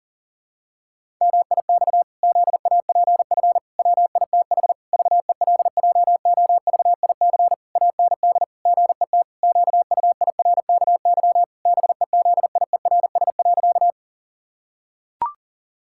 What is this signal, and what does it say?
Morse code, heard as a keyed tone, about 37 wpm, 700 Hz. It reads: MIX ZAPF WITH VELJOVIC AND GET QUIRKY BEZIERS.
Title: Pangram Challenge v2 - 40wpm